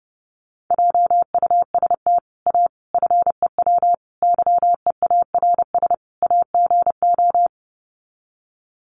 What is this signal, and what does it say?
Morse code, heard as a keyed tone, about 30 wpm, 700 Hz. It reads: JUST A FEW YEARS AGO
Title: Sets of 5 Words Encore 30wpm